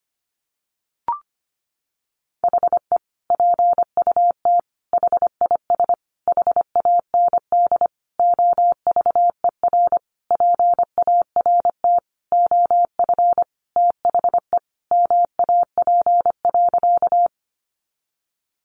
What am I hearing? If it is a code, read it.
Morse code, heard as a keyed tone, about 25 wpm, 700 Hz. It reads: HE PUT HIS HAND OVER PART OF THE MAP.
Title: Sentences from Top 400 Words 25wpm